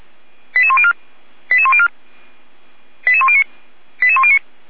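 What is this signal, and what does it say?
Five-tone selective call (ZVEI), heard as quick runs of five tones, tones of about 70 ms. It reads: 88104 88104 88109 88109
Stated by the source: ZVEI five-tone call received by SDR, Feuerwehr Orlamünde.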